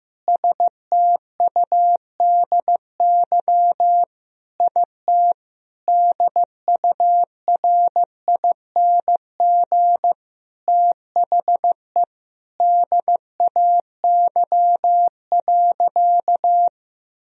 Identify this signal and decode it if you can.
Morse code, heard as a keyed tone, about 15 wpm, 700 Hz. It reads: STUDY IT DURING THE DAY.
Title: Sentences from Top 1000 Words - Review Part 2 - 15wpm